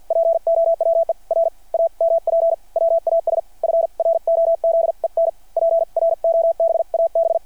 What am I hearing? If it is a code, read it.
Morse code, heard as a keyed tone, about 52 wpm, spaced as for 43 wpm, 660 Hz. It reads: POWER AMP WAS FROZEN PROBAB
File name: archived / CRSnet